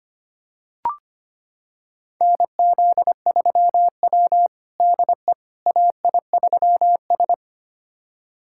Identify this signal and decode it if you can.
Morse code, heard as a keyed tone, about 25 wpm, 700 Hz. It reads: NZ3W DE AI3S
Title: USA Callsign DE Callsign - 4 Characters - 25wpm